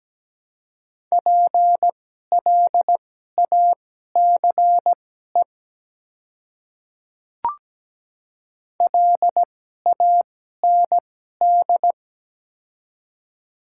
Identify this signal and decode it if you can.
Morse code, heard as a keyed tone, about 17 wpm, spaced as for 12 wpm, 700 Hz. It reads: PLACE LAND
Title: 2x Character Spacing - Top 200 Words - 17wpm